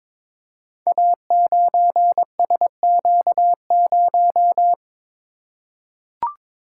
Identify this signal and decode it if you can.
Morse code, heard as a keyed tone, about 22 wpm, 700 Hz. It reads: A9SQ0
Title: Code Groups - 5 Letter:Numbers - 22wpm